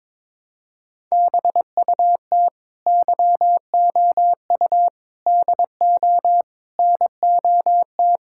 Morse code, heard as a keyed tone, about 21 wpm, 700 Hz. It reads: BUT YOU DO NOT
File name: practice / 22wpm / Sentences with 4-Word Phrases - 22wpm